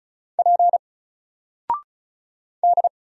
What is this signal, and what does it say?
Morse code, heard as a keyed tone, about 35 wpm, 700 Hz. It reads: P D
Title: Single Letter-Number 35wpm